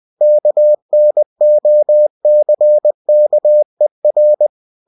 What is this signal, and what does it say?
Morse code, heard as a keyed tone, about 20 wpm, 600 Hz. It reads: KNOCKER
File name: cw-books / winnie-the-pooh / winnie-20wpm-page-0028